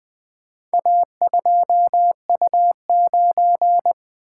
Morse code, heard as a keyed tone, about 20 wpm, 700 Hz. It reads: A2U9
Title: Four Letter-Number Combinations - 20wpm